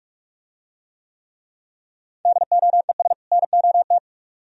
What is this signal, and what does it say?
Morse code, heard as a keyed tone, about 44 wpm, 700 Hz. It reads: DOES NOT